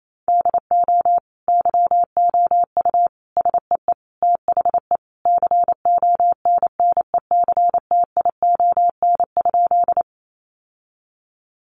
Morse code, heard as a keyed tone, about 28 wpm, 700 Hz. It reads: DO YOU SEE THE CONNECTION?
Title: Sentences from Top 1300 Words 28wpm